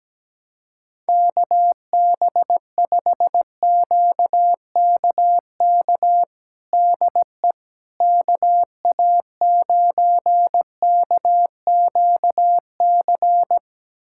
Morse code, heard as a keyed tone, about 17 wpm, 700 Hz. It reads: KB5QKK DE KA9KQC
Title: USA Callsign DE Callsign - 6 Characters - 17wpm